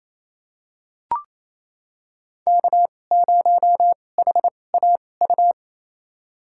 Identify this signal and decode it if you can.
Morse code, heard as a keyed tone, about 28 wpm, spaced as for 20 wpm, 700 Hz. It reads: K0HAU
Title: USA Call Signs - 5 Characters - 2x Character Spacing - 28wpm